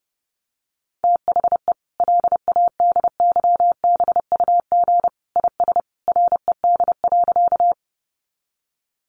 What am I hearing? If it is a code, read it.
Morse code, heard as a keyed tone, about 30 wpm, 700 Hz. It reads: THE LADYBUG IS RED.
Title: Additional Easy Phrases 30wpm